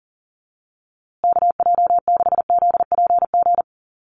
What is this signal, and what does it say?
Morse code, heard as a keyed tone, about 40 wpm, 700 Hz. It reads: KJ6ZPG
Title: USA Call Signs Repeated 3x 40wpm